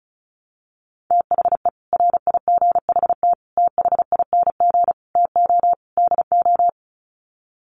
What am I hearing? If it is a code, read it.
Morse code, heard as a keyed tone, about 34 wpm, 700 Hz. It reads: THE RIGHT THING TO DO
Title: Sets of 5 Words 35wpm